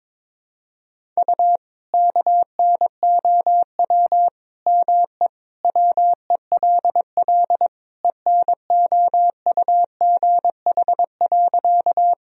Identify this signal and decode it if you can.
Morse code, heard as a keyed tone, about 22 wpm, 700 Hz. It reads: U KNOW ME WELL ENOUGH.
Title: Sentences from Top 400 Words and Abbreviations 22wpm